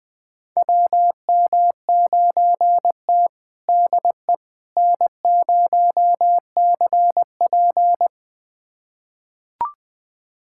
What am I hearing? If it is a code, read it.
Morse code, heard as a keyed tone, about 21 wpm, 700 Hz. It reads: WM9T DE N0CP
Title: USA Callsign DE Callsign - 4 Characters - 20wpm